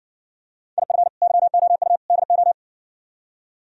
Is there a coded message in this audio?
Yes: Morse code from 0.8 to 2.5 s, heard as a keyed tone, about 65 wpm, 700 Hz. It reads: IF YOU DO